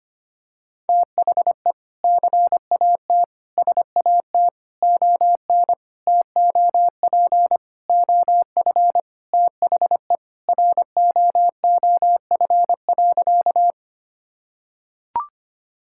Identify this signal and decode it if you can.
Morse code, heard as a keyed tone, about 25 wpm, 700 Hz. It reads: THE CAT SAT ON TOP OF THE ROOF.
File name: practice / 25wpm / Sentences with 3-Word Phrases - 25wpm